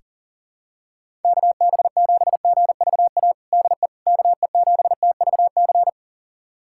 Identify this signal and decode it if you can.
Morse code, heard as a keyed tone, about 37 wpm, 700 Hz. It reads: KB7GVA DE KE7TVC
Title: USA Callsign DE Callsign - 6 Characters - 40wpm